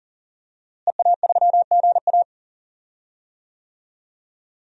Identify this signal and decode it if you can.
Morse code, heard as a keyed tone, about 40 wpm, 700 Hz. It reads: EA3GA
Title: DXCC-100 Call Signs - 5 Characters - Repeated 3x - 40wpm